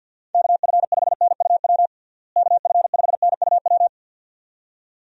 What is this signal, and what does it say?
Morse code, heard as a keyed tone, about 52 wpm, 700 Hz. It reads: KF5NUW KF5NUW
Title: USA Call Signs - 6 Characters - Repeated 2x - 50wpm